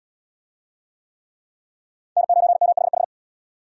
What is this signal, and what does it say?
Morse code, heard as a keyed tone, about 69 wpm, 700 Hz. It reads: N9MHH